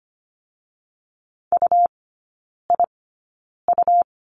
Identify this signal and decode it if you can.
Morse code, heard as a keyed tone, about 25 wpm, 700 Hz. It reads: U I U
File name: course / 25:10wpm / 100 Lesson 17e - Letter u - Three Characters 25wpm